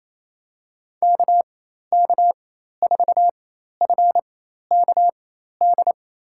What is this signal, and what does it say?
Morse code, heard as a keyed tone, about 28 wpm, spaced as for 13 wpm, 700 Hz. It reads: KK4FKD
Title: USA Call Signs - 6 Characters - 4x Character Spacing - 28wpm